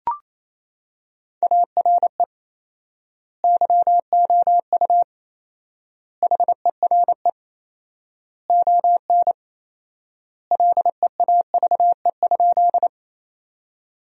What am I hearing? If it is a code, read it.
Morse code, heard as a keyed tone, about 28 wpm, 700 Hz. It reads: ARE YOU HERE ON LEAVE?
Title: 4x Word-Spacing - Sentences from Top 400 Words - 28wpm